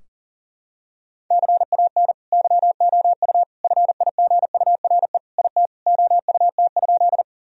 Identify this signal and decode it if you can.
Morse code, heard as a keyed tone, about 40 wpm, 700 Hz. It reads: CAN YOU FIGURE IT OUT?
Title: Sentences from Top 1000 Words - Review Part 2 - 40wpm